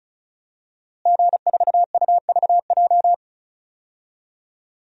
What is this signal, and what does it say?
Morse code, heard as a keyed tone, about 36 wpm, 700 Hz. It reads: G4UVJ
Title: DXCC-100 Call Signs - 5 Characters - Repeated 3x - 35wpm